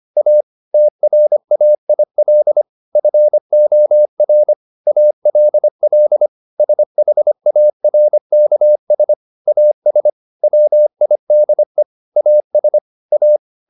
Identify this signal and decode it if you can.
Morse code, heard as a keyed tone, about 25 wpm, 600 Hz. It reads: A TRAIL FOR ALL SHARKS AS WIDE AS A